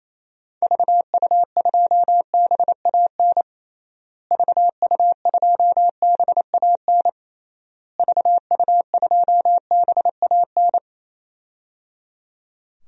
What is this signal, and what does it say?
Morse code, heard as a keyed tone, about 28 wpm, 700 Hz. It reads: VU2BAN VU2BAN VU2BAN
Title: DXCC-100 Call Signs - 6 Characters - Repeated 3x - 28wpm